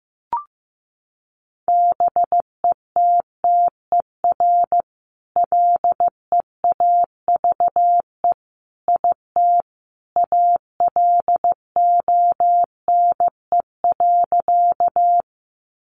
Morse code, heard as a keyed tone, about 15 wpm, 700 Hz. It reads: BETTER LEAVE IT ALONE.
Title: Sentences from Top 1000 Words 15wpm